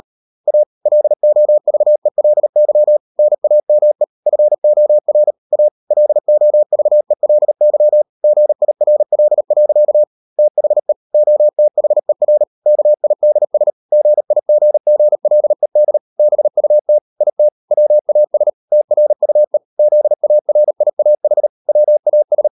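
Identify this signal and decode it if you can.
Morse code, heard as a keyed tone, about 38 wpm, 600 Hz. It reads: A LOVELY NAME FOR A LOVELY GIRL. THE OTHER KIDS GIGGLED BUT IT WAS TRUE ZARIAH WAS
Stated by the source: Morse code practice at 38 WPM, sending Ken Follett's novel Never in English, from CW med Gnister.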